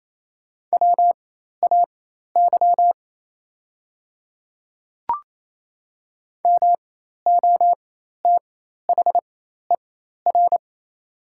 Morse code, heard as a keyed tone, about 28 wpm, spaced as for 13 wpm, 690 Hz. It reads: WAY MOTHER
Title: 4x Character Spacing - Top 200 Words - 28wpm